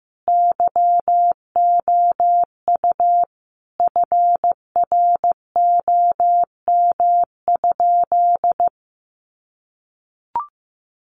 Morse code, heard as a keyed tone, about 15 wpm, 700 Hz. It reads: YOU FROM?
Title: Sentences from Top 200 Words 15wpm